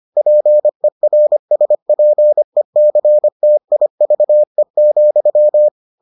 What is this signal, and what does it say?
Morse code, heard as a keyed tone, about 25 wpm, 600 Hz. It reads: PERSPECTIVE,